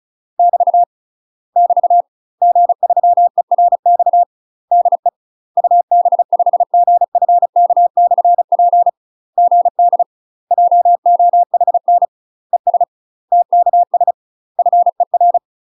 Morse code, heard as a keyed tone, about 34 wpm, 700 Hz. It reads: <BT> <BT> G3ERX DE UB5GFK/P GD JOHN ES TKS FER
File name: UK MASTER QSOs TRAINING / 35WPM / UK MASTER QSOs 0004